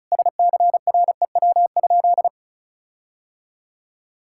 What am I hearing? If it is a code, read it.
Morse code, heard as a keyed tone, about 36 wpm, 700 Hz. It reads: SCREW?
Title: Sentences from Top 1600 Words 35wpm